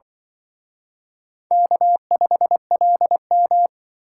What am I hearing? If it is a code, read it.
Morse code, heard as a keyed tone, about 25 wpm, 700 Hz. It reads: K5LM
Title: Speed Racing - USA 1x2 Call Signs 20wpm